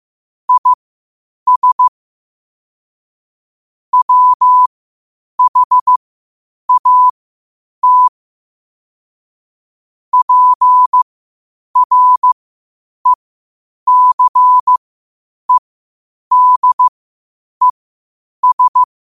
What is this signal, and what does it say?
Morse code, heard as a keyed tone, about 15 wpm, spaced as for 8 wpm, 1000 Hz. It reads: IS WHAT PRECEDES
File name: 8wpm